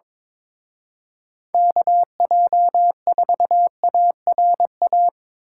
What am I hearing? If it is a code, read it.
Morse code, heard as a keyed tone, about 25 wpm, 700 Hz. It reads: KJ4ARA
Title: Speed Racing - USA 2x3 Call Signs 22wpm